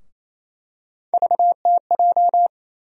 Morse code, heard as a keyed tone, about 28 wpm, 700 Hz. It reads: VTJ
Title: USA Call Sign Postfixes - 3 Letters 28wpm